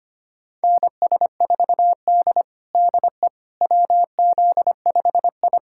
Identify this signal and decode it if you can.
Morse code, heard as a keyed tone, about 25 wpm, 700 Hz. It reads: NS4D DE WZ5I
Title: USA Callsign DE Callsign - 4 Characters - 25wpm